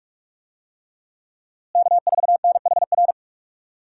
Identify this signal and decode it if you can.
Morse code, heard as a keyed tone, about 44 wpm, 700 Hz. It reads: K4NHR
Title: USA Call Signs - 5 Characters - Repeated 3x - 45wpm